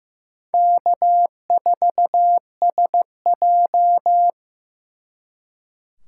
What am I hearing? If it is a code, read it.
Morse code, heard as a keyed tone, about 18 wpm, 700 Hz. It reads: K4SJ
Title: Speed Racing - USA 1x2 Call Signs 15wpm